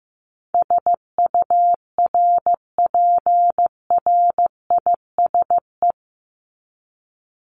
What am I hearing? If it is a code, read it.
Morse code, heard as a keyed tone, about 15 wpm, 700 Hz. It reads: SURPRISE
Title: Top 1751-2000 Words 15wpm